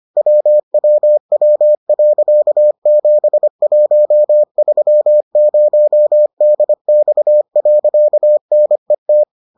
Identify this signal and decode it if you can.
Morse code, heard as a keyed tone, about 25 wpm, 600 Hz. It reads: WWW.7130DX.NET